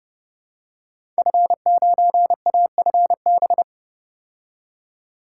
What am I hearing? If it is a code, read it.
Morse code, heard as a keyed tone, about 30 wpm, 700 Hz. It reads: F9AFB
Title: Code Groups - 5 Letter:Numbers - 30wpm